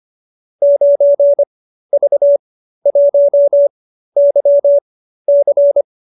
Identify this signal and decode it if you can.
Morse code, heard as a keyed tone, about 25 wpm, spaced as for 13 wpm, 570 Hz. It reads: 9V1YC